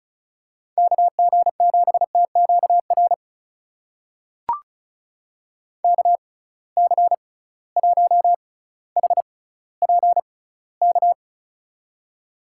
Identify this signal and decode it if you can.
Morse code, heard as a keyed tone, about 36 wpm, spaced as for 18 wpm, 700 Hz. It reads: KG7TQR KC1HPK
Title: USA Call Signs - 6 Characters - 6x Character Spacing - 35wpm